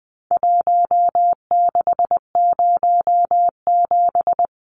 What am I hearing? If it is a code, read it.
Morse code, heard as a keyed tone, about 20 wpm, 700 Hz. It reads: 1607
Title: Numbers - Contest Style - 4-Digit 20wpm